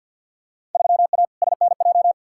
Morse code, heard as a keyed tone, about 53 wpm, 700 Hz. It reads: 3A SNJ
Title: Field Day Exchanges - 50wpm